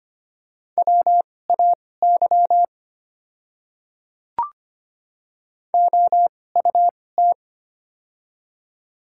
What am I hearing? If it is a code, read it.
Morse code, heard as a keyed tone, about 25 wpm, spaced as for 18 wpm, 700 Hz. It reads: WAY OUT
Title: 2x Character Spacing - Top 100 Words - 25wpm